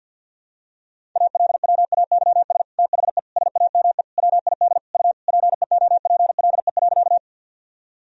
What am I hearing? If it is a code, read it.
Morse code, heard as a keyed tone, about 47 wpm, 700 Hz. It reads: ALWAYS THE SAME WID U PEOPLE.
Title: Sentences from Top 300 Words and Abbreviations 50wpm